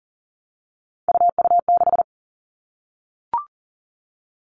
Morse code, heard as a keyed tone, about 38 wpm, 700 Hz. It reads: UU6
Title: All Cut Numbers - 3-Digit 40wpm